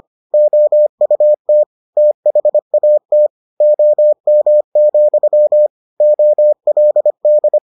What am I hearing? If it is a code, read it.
Morse code, heard as a keyed tone, about 25 wpm, 600 Hz. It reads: OUT THAT OM, OLD